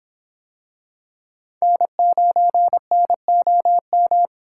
Morse code, heard as a keyed tone, about 23 wpm, 700 Hz. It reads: N9NOM